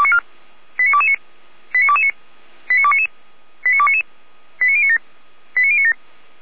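Five-tone selective call (ZVEI), heard as quick runs of five tones, tones of about 70 ms. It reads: 89229 89229 89299 89299 89097 89097
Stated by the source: ZVEI five-tone call received by SDR, archived by Feuerwehr Orlamünde.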